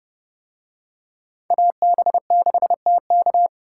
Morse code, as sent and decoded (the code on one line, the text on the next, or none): .- -... -.... - -.-
AB6TK